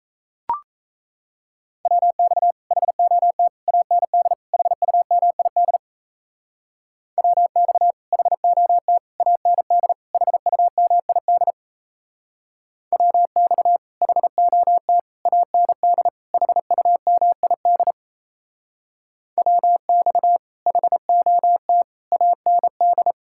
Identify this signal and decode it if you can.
Morse code, heard as a keyed tone, about 34 wpm, 700 Hz. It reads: WX HOT AND HUMID WX HOT AND HUMID WX HOT AND HUMID WX HOT AND
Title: Speed-Racing Common QSO Phrases - 28wpm